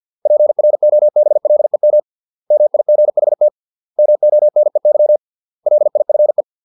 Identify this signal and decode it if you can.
Morse code, heard as a keyed tone, about 50 wpm, 600 Hz. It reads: PROBLEM MIGHT MONEY LIFE